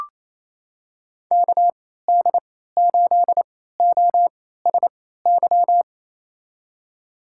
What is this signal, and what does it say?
Morse code, heard as a keyed tone, about 28 wpm, spaced as for 16 wpm, 700 Hz. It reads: KD8OSY
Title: USA Call Signs - 6 Characters - 3x Character Spacing - 28wpm